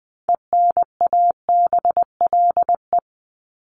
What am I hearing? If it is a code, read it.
Morse code, heard as a keyed tone, about 20 wpm, 700 Hz. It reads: ENABLE
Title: Top 1251-1500 Words 20wpm